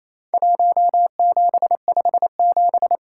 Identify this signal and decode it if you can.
Morse code, heard as a keyed tone, about 28 wpm, 700 Hz. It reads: 1757
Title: Worked All Europe DX Contest - QTC Exchange - 28wpm